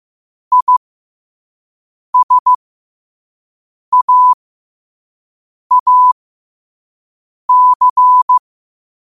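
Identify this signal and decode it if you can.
Morse code, heard as a keyed tone, about 15 wpm, spaced as for 5 wpm, 1000 Hz. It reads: ISAAC
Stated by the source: Morse quote-of-the-day practice at 5 wpm.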